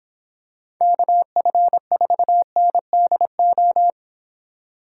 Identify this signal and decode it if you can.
Morse code, heard as a keyed tone, about 25 wpm, 700 Hz. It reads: KF4NDO